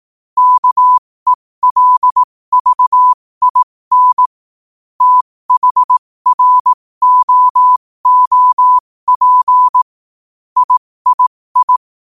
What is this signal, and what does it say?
Morse code, heard as a keyed tone, about 18 wpm, spaced as for 15 wpm, 1000 Hz. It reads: KELVIN THROOP III
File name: qotdfiles / 15wpm